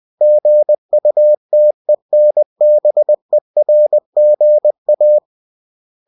Morse code, heard as a keyed tone, about 20 wpm, 600 Hz. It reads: GUTENBERGA